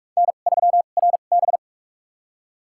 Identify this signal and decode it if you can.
Morse code, heard as a keyed tone, about 47 wpm, spaced as for 33 wpm, 700 Hz. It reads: N3RB
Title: USA Call Signs - 4 Characters - 2x Character Spacing - 45wpm